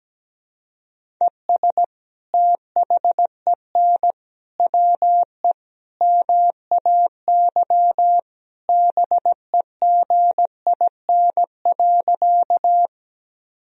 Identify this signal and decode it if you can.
Morse code, heard as a keyed tone, about 17 wpm, 700 Hz. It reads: ES THEN WE MAY BEGIN.
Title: Sentences from Top 300 Words and Abbreviations 17wpm